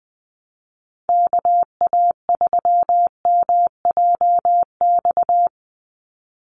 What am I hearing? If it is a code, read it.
Morse code, heard as a keyed tone, about 20 wpm, 700 Hz. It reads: KA3MJX
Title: USA Call Signs 20wpm